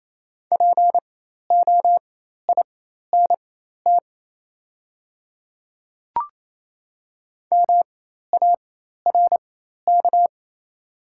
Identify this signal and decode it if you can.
Morse code, heard as a keyed tone, about 27 wpm, spaced as for 13 wpm, 700 Hz. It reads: POINT MARK
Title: 4x Character Spacing - Top 500 Words - 28wpm